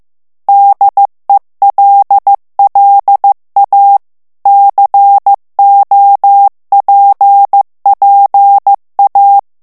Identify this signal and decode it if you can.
Morse code, heard as a keyed tone, about 15 wpm, 800 Hz. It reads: DELLA COPPA
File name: morse mp3 / ITALIEN texte clair 2 vitesse 14 mots minute